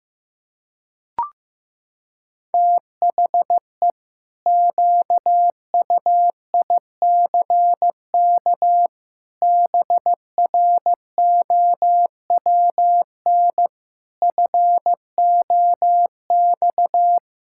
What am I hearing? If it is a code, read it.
Morse code, heard as a keyed tone, about 15 wpm, 700 Hz. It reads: THE QUICK BROWN FOX
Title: Pangram Challenge v2 - 15wpm